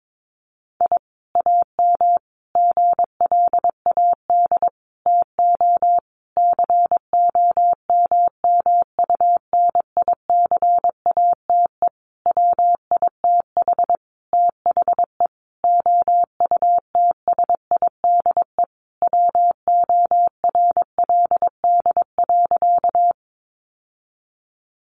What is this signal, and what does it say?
Morse code, heard as a keyed tone, about 22 wpm, 700 Hz. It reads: I AM GLAD TO COMMUNICATE WITH THE OUTSIDE WORLD.
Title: Sentences from Top 1300 Words 22wpm